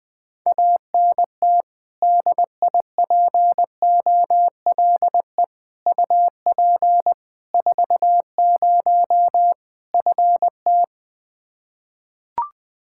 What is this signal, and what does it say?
Morse code, heard as a keyed tone, about 20 wpm, 700 Hz. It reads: ANT DIPOLE UP 40 FT
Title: Practice QSO - Exchange 2 - Info - 20wpm